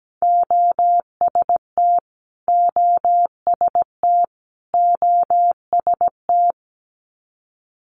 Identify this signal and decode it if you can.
Morse code, heard as a keyed tone, about 17 wpm, 700 Hz. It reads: OST OST OST
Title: Multi-letter Phonemes Repeated 3x 17wpm